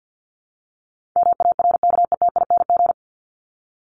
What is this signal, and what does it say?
Morse code, heard as a keyed tone, about 54 wpm, 700 Hz. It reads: MARKETING
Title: ING Suffix - Top 100 Words with Base Words 50wpm